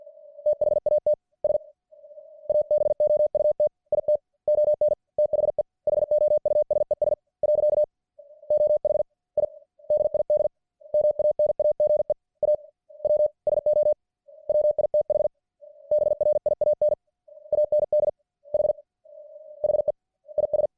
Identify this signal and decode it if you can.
Morse code, heard as a keyed tone, about 50 wpm, 600 Hz. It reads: THAT S ABOUT IT ON THE HOUSES . OH I DID MANAGE A W SO WITH BRIAN AND H HE IS